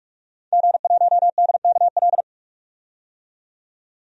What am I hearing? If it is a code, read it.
Morse code, heard as a keyed tone, about 46 wpm, 700 Hz. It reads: G1DKL
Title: DXCC-100 Call Signs - 5 Characters - 45wpm